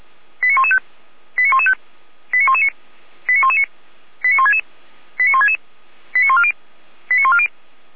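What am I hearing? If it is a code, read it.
Five-tone selective call (ZVEI), heard as quick runs of five tones, tones of about 70 ms. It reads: 89115 89115 89119 89119 89166 89166 89140 89140
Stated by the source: Five-tone ZVEI selective call received by SDR.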